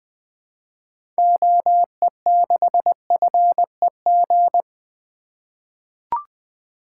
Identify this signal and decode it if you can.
Morse code, heard as a keyed tone, about 20 wpm, 700 Hz. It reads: OE6FEG
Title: 2023 SOTA Activator Call Signs - 20wpm